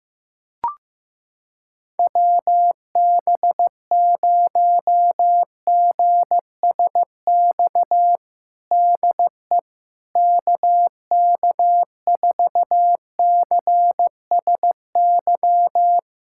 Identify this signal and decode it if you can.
Morse code, heard as a keyed tone, about 16 wpm, 700 Hz. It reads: WB0GSX DE KK4CSY
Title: USA Callsign DE Callsign - 6 Characters - 15wpm